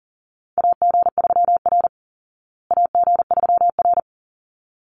Morse code, heard as a keyed tone, about 40 wpm, 700 Hz. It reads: AG3R AG3R
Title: Call Signs Repeated 3x - 40wpm